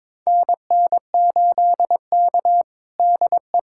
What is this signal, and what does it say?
Morse code, heard as a keyed tone, about 22 wpm, 700 Hz. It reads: NN8K DE
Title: USA Callsign DE Callsign - 4 Characters - 22wpm